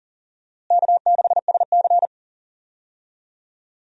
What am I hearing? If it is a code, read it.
Morse code, heard as a keyed tone, about 40 wpm, 700 Hz. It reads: K6SC
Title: USA Call Signs - 4 Characters - Repeated 2x - 40wpm